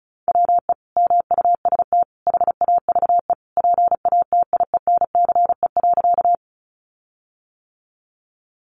Morse code, heard as a keyed tone, about 34 wpm, 700 Hz. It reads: WE MUST HAVE PATIENCE.
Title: Sentences from Top 1300 Words 35wpm